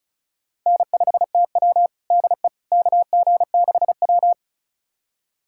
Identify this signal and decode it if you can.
Morse code, heard as a keyed tone, about 36 wpm, 700 Hz. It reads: N5TW DE KG6W
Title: USA Callsign DE Callsign - 4 Characters - 35wpm